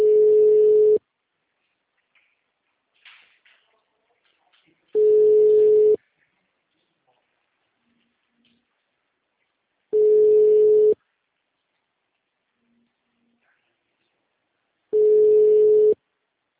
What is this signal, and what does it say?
Ringback tone (Europe, 425 Hz).